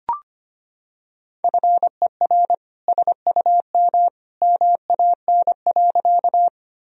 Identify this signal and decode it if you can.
Morse code, heard as a keyed tone, about 25 wpm, 700 Hz. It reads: FER SUM MAN.
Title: Sentences from Top 200 Words and Abbreviations 25wpm